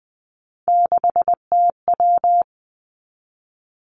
Morse code, heard as a keyed tone, about 20 wpm, 700 Hz. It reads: 6TW
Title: Three Challenging Letter-Number Combinations 20wpm